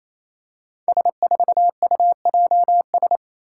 Morse code, heard as a keyed tone, about 28 wpm, 700 Hz. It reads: S4UJS
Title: Five Challenging Letter-Number Combinations 28wpm